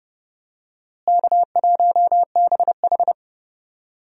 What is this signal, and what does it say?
Morse code, heard as a keyed tone, about 32 wpm, 700 Hz. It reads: K1BH
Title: USA Call Signs - 4 Characters - Repeated 2x - 30wpm